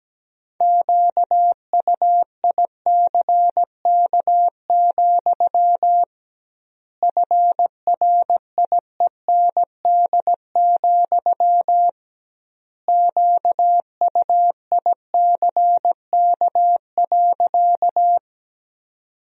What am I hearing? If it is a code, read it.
Morse code, heard as a keyed tone, about 17 wpm, 700 Hz. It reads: QUICK, FRIEND, QUICK.
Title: 2x Word-Spacing - Sentences from Top 500 Words - 17wpm